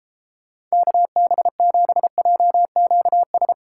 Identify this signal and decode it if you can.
Morse code, heard as a keyed tone, about 32 wpm, 700 Hz. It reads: KB7JQS